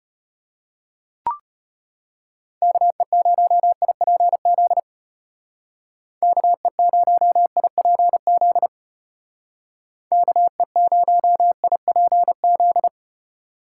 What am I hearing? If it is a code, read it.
Morse code, heard as a keyed tone, about 30 wpm, 700 Hz. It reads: KE0IPZ KE0IPZ KE0IPZ
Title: Speed Racing - USA 2x3 Call Signs 25wpm